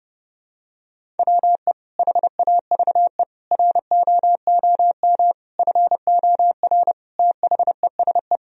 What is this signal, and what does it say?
Morse code, heard as a keyed tone, about 30 wpm, 700 Hz. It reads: WE HAVE ROOM FOR THESE
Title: Sentences from Top 1000 Words - Review Part 1 - 30wpm